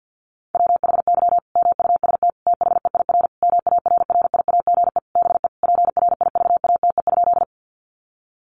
Morse code, heard as a keyed tone, about 50 wpm, 700 Hz. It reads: WHY MUST THEIR MARRIAGE BE PRIVATE?
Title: Sentences from Top 1000 Words 50wpm